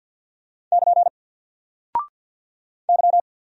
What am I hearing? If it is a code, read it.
Morse code, heard as a keyed tone, about 48 wpm, 700 Hz. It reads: <KN> Y
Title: Single Character (Letter:Number:Punctuation:Prosign) 50wpm